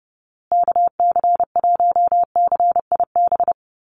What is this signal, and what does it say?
Morse code, heard as a keyed tone, about 30 wpm, 700 Hz. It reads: KC1CIB
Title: USA Call Signs Repeated 3x 30wpm